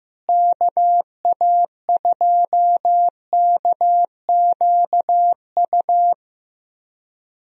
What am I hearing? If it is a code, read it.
Morse code, heard as a keyed tone, about 15 wpm, 700 Hz. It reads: KA2KQU
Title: USA Call Signs - 6 Characters - Repeated 2x - 15wpm